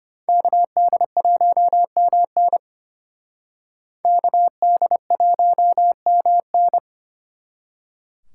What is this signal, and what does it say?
Morse code, heard as a keyed tone, about 32 wpm, 700 Hz. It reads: KD1MN KD1MN